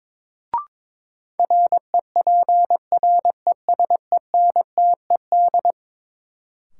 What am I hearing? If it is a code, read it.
Morse code, heard as a keyed tone, about 22 wpm, 700 Hz. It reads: REPRESENTED
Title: Sets of 5 Words - New Word Drill - 22wpm